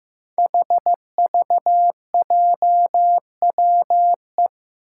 Morse code, heard as a keyed tone, about 15 wpm, 700 Hz. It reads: HVJWE
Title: Five Challenging Letter-Number Combinations 15wpm